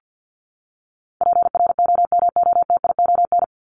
Morse code, heard as a keyed tone, about 50 wpm, 700 Hz. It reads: PROMOTION
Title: Top 1501-1750 Words 50wpm